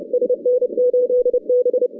Morse code, heard as a keyed tone, about 30 wpm, 500 Hz. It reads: SN8B
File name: 2015 cq ww cw / 2015 cqwww k5tr - 0-31